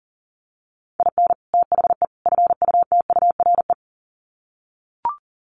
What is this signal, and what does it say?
Morse code, heard as a keyed tone, about 40 wpm, 700 Hz. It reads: IN THE FUTURE
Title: Sets of 3 Words 40wpm